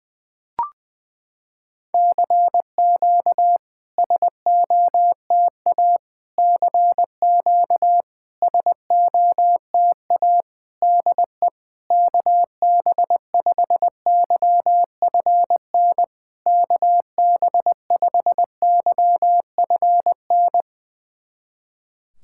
Morse code, heard as a keyed tone, about 20 wpm, 700 Hz. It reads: CQ SOTA CQ SOTA DE KB5YFN KB5YFN
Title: SOTA QSO Practice 20wpm